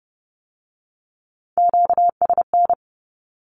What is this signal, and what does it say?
Morse code, heard as a keyed tone, about 30 wpm, 700 Hz. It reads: QSN
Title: Q-Codes with Spoken Meaning - 30wpm